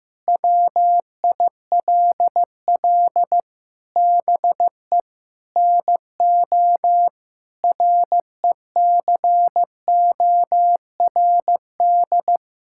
Morse code, heard as a keyed tone, about 15 wpm, 700 Hz. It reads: WILL BE NO RECORD
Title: Sentences from Top 1000 Words - Review Part 5 - 15wpm